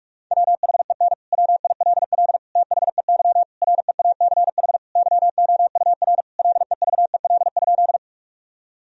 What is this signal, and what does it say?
Morse code, heard as a keyed tone, about 42 wpm, 700 Hz. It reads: WHEN WILL THEY REACH YOUR LEVEL?
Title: Sentences from Top 1000 Words - Review Part 4 - 45wpm